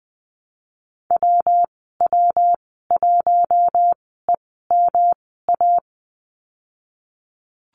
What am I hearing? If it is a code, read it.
Morse code, heard as a keyed tone, about 20 wpm, spaced as for 15 wpm, 700 Hz. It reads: WW1EMA
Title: 128 Lesson 21g - Number 6 - Call Signs 20wpm